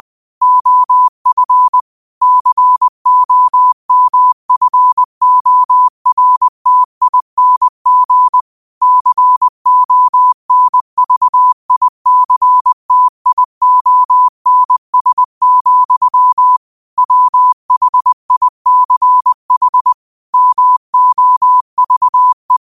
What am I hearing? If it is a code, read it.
Morse code, heard as a keyed tone, about 20 wpm, 1000 Hz. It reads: OF COMFORTING CONVICTIONS, WHICH MOVE
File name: qotdfiles / 20wpm